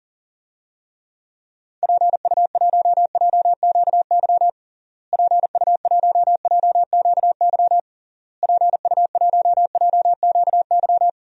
Morse code, heard as a keyed tone, about 40 wpm, 700 Hz. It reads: PU1JQY PU1JQY PU1JQY
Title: DXCC-100 Call Signs - 6 Characters - Repeated 3x - 40wpm